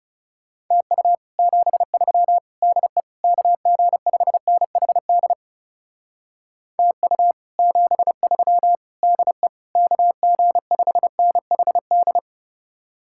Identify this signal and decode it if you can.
Morse code, heard as a keyed tone, about 38 wpm, 700 Hz. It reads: TU 73 DE KG5NHD TU 73 DE KG5NHD
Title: Speed-Racing Common QSO Phrases Encore - 30wpm